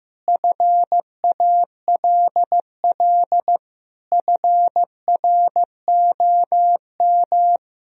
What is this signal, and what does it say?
Morse code, heard as a keyed tone, about 15 wpm, 700 Hz. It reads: FALL FROM
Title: Sentences from Top 1000 Words - Review Part 2 - 15wpm